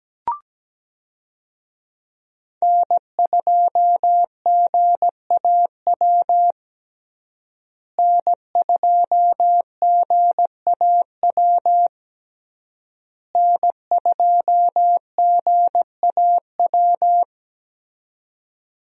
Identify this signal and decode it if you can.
Morse code, heard as a keyed tone, about 17 wpm, 700 Hz. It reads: N2GAW N2GAW N2GAW
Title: USA Call Signs - 5 Characters - Repeated 3x - 17wpm